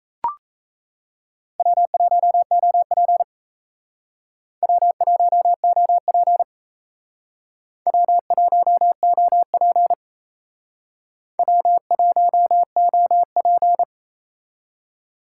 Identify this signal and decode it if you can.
Morse code, heard as a keyed tone, about 35 wpm, 700 Hz. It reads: W1OP W1OP W1OP W1OP